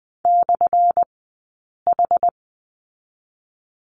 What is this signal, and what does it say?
Morse code, heard as a keyed tone, about 20 wpm, 700 Hz. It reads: / H
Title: 177 Lesson 29d - Slash - Two Characters 20wpm